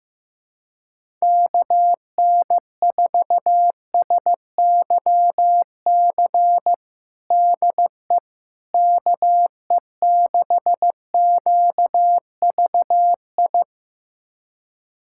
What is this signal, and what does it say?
Morse code, heard as a keyed tone, about 15 wpm, 700 Hz. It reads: KN4SYC DE KE6QVI
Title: USA Callsign DE Callsign - 6 Characters - 15wpm